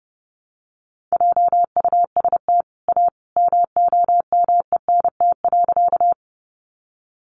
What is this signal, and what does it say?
Morse code, heard as a keyed tone, about 28 wpm, 700 Hz. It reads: JUST A MOMENT.
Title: Sentences from Top 1000 Words 30wpm